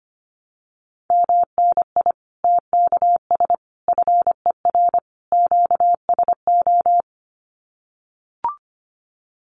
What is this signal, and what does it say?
Morse code, heard as a keyed tone, about 25 wpm, 700 Hz. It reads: MNI TKS FER QSO
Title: Common QSO Phrases 25wpm